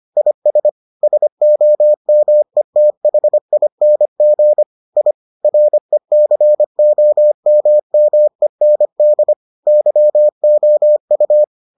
Morse code, heard as a keyed tone, about 25 wpm, 600 Hz. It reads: IS SOMETHING I RECOMMEND YOU